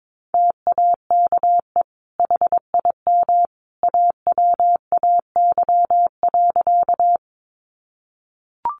Morse code, heard as a keyed tone, about 22 wpm, 700 Hz. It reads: TAKE HIM AWAY.
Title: Sentences from Top 600 Words 22wpm